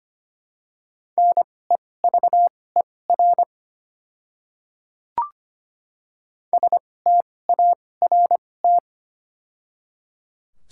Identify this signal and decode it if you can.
Morse code, heard as a keyed tone, about 25 wpm, spaced as for 18 wpm, 700 Hz. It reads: NEVER START
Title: 2x Character Spacing - Top 300 Words - 25wpm